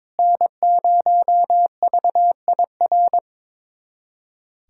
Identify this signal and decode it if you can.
Morse code, heard as a keyed tone, about 22 wpm, 700 Hz. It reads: N0VIR
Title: USA Callsign DE Callsign - 5 Characters - 22wpm